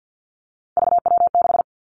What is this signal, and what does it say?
Morse code, heard as a keyed tone, about 50 wpm, 700 Hz. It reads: VW6
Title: Three Challenging Letter-Number Combinations 50wpm